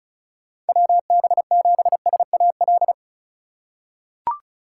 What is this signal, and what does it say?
Morse code, heard as a keyed tone, about 34 wpm, 700 Hz. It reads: WB7SAL
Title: USA Call Signs - 6 Characters - 3x Character Spacing - 35wpm